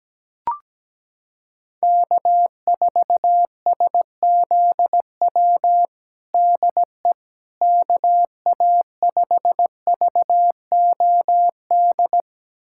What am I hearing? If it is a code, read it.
Morse code, heard as a keyed tone, about 17 wpm, 700 Hz. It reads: K4SZW DE KA5VOD
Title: USA Callsign DE Callsign 17wpm